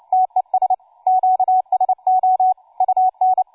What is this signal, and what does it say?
Morse code, heard as a keyed tone, about 29 wpm, 750 Hz, with radio noise. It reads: TES QSO UN
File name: cw forum / tu2 f5ldy 18mhz 16h46